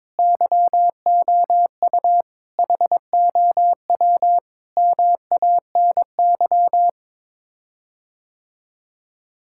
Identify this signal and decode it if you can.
Morse code, heard as a keyed tone, about 22 wpm, 700 Hz. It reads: YOU HOW MANY
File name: practice / 22wpm / Sets of 6 Words - 22wpm